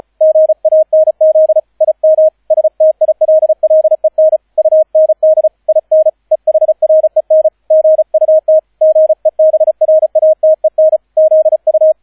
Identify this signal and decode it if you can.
Morse code, heard as a keyed tone, about 35 wpm, 630 Hz, with radio noise. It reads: GANZ IM STILLEN UND IN EHREN GUT GEBRATEN ZU